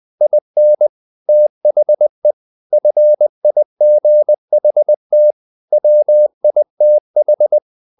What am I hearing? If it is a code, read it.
Morse code, heard as a keyed tone, about 20 wpm, 600 Hz. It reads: IN THE FIGHT WITH
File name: the old man and the sea part 2 20wpm